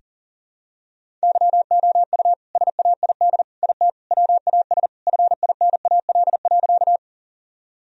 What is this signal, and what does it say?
Morse code, heard as a keyed tone, about 37 wpm, 700 Hz. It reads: YOU SAID IT WAS FINAL.
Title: Sentences from Top 1000 Words - Review Part 2 - 40wpm